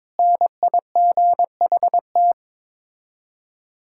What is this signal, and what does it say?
Morse code, heard as a keyed tone, about 22 wpm, 700 Hz. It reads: NIGHT
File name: practice / 22wpm / Sets of 8 Words - 22wpm